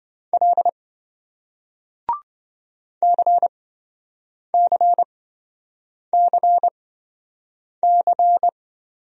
Morse code, heard as a keyed tone, about 26 wpm, spaced as for 7 wpm, 700 Hz. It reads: L CCCC